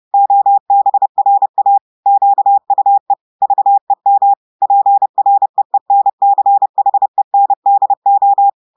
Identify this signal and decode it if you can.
Morse code, heard as a keyed tone, about 30 wpm, 800 Hz. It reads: OBRA QUE VEM PREENCHENDO
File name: audios / portugais / portugais texte en clair 1 vitesse 30 mots minutes